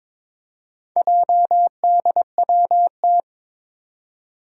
Morse code, heard as a keyed tone, about 23 wpm, 700 Hz. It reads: JDWT
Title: Four Challenging Letter-Number Combinations 22wpm